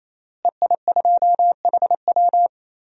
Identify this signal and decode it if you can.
Morse code, heard as a keyed tone, about 28 wpm, 700 Hz. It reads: EI2HW
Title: DXCC-100 Call Signs - 5 Characters - Repeated 3x - 28wpm